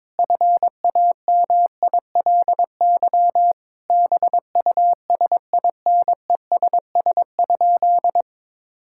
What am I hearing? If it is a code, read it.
Morse code, heard as a keyed tone, about 22 wpm, 700 Hz. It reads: FAMILY BUSINESS?
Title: Sentences from Top 1000 Words - Review Part 6 - 22wpm